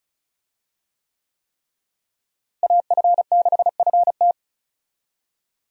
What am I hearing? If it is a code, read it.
Morse code, heard as a keyed tone, about 36 wpm, 700 Hz. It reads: AF6FT
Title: USA Call Signs - 5 Characters - Repeated 3x - 35wpm